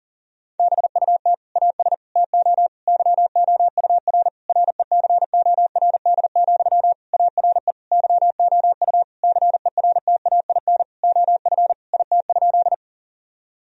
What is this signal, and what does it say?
Morse code, heard as a keyed tone, about 40 wpm, 700 Hz. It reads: BUT AS TO YOUR RECORD, ARE YOU CERTAIN OF IT?
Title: Sentences from Top 1000 Words - Review Part 3 - 40wpm